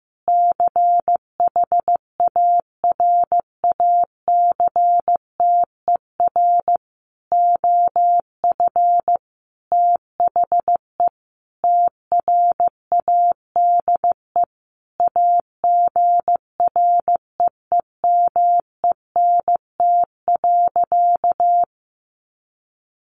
Morse code, heard as a keyed tone, about 14 wpm, 700 Hz. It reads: CHARACTER OF THE TRADE AGREEMENT.
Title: Sentences from Top 1500 Words 15wpm